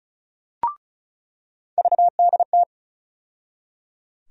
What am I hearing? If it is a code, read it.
Morse code, heard as a keyed tone, about 36 wpm, 700 Hz. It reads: VDT
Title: USA Call Sign Postfixes - 3 Letters 35wpm